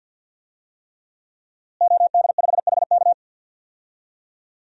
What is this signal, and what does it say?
Morse code, heard as a keyed tone, about 46 wpm, 700 Hz. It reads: OD5HK
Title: DXCC-100 Call Signs - 5 Characters - Repeated 3x - 50wpm